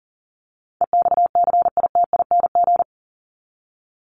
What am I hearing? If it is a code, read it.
Morse code, heard as a keyed tone, about 40 wpm, 700 Hz. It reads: EXCITING
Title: Top 1751-2000 Words 40wpm